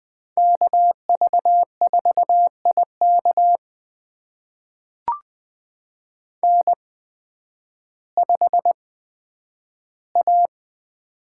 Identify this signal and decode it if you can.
Morse code, heard as a keyed tone, about 20 wpm, spaced as for 11 wpm, 700 Hz. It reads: KV4IK N5A
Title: USA Call Signs - 5 Characters - 8x Character Spacing - 20wpm